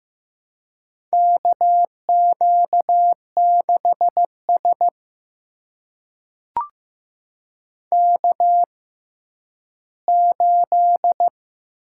Morse code, heard as a keyed tone, about 15 wpm, spaced as for 10 wpm, 700 Hz. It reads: KQ6S K8
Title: USA Call Signs - 4 Characters - 6x Character Spacing - 15wpm